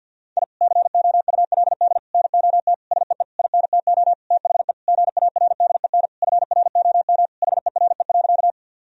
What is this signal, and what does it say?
Morse code, heard as a keyed tone, about 50 wpm, 700 Hz. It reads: I COULD NOT SEE INTO THE GARDEN FROM HERE.
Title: Sentences from Top 1000 Words - Review Part 4 - 50wpm